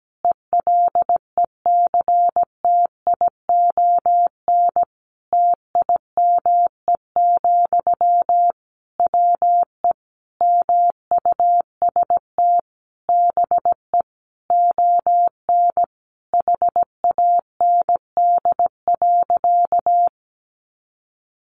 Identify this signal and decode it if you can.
Morse code, heard as a keyed tone, about 17 wpm, 700 Hz. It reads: ELECTION TIME, WE MUST BE ON HAND.
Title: Sentences from Top 1500 Words 17wpm